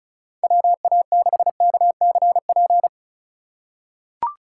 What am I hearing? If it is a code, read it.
Morse code, heard as a keyed tone, about 36 wpm, 700 Hz. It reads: WA6KCP
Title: USA Call Signs - 6 Characters - 4x Character Spacing - 35wpm